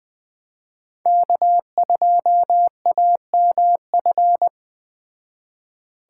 Morse code, heard as a keyed tone, about 20 wpm, 700 Hz. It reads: K2AMF